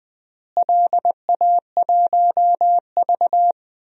Morse code, heard as a keyed tone, about 20 wpm, 700 Hz. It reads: LA1V
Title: DXCC-100 Call Signs - 4 Characters - Repeated 3x - 20wpm